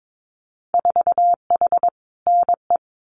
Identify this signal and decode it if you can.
Morse code, heard as a keyed tone, about 22 wpm, 700 Hz. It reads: 4H NE
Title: Winter Field Day Exchanges Encore 22wpm